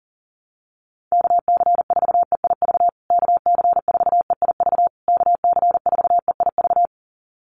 Morse code, heard as a keyed tone, about 40 wpm, 700 Hz. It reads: KC4EIV KC4EIV KC4EIV
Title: USA Call Signs Repeated 3x 40wpm